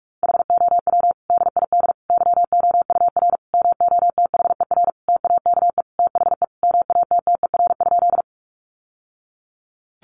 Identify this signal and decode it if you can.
Morse code, heard as a keyed tone, about 44 wpm, 700 Hz. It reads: HOW DID YOUR MOTHER TAKE THE MATTER?
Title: Sentences from Top 700 Words 45wpm